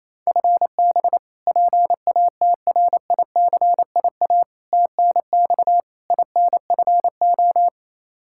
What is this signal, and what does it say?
Morse code, heard as a keyed tone, about 28 wpm, 700 Hz. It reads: FB PATRICIA TNX INFO
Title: Practice QSO - Exchange 3 - More Info - 28wpm